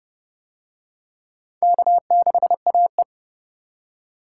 Morse code, heard as a keyed tone, about 28 wpm, 700 Hz. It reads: K6AE